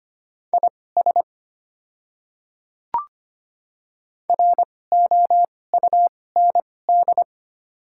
Morse code, heard as a keyed tone, about 25 wpm, spaced as for 18 wpm, 700 Hz. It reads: IS ROUND
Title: 2x Character Spacing - Top 300 Words - 25wpm